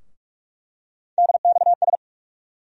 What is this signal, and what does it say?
Morse code, heard as a keyed tone, about 42 wpm, 700 Hz. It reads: DXS
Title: USA Call Sign Postfixes - 3 Letters 45wpm